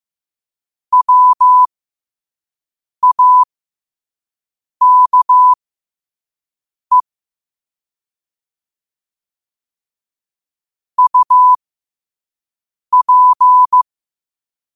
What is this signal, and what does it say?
Morse code, heard as a keyed tone, about 15 wpm, spaced as for 5 wpm, 1000 Hz. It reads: WAKE UP
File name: qotdfiles / 5wpm